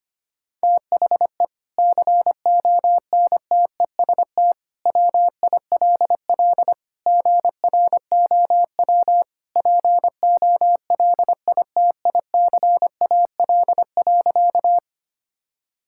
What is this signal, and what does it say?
Morse code, heard as a keyed tone, about 25 wpm, 700 Hz. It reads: THE CONTEST WILL GROW POLITICAL.
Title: Sentences from Top 1600 Words 25wpm